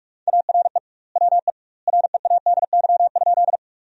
Morse code, heard as a keyed tone, about 45 wpm, 700 Hz. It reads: ARE WE READY?
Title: 2x Word-Spacing - Sentences from Top 400 Words - 45wpm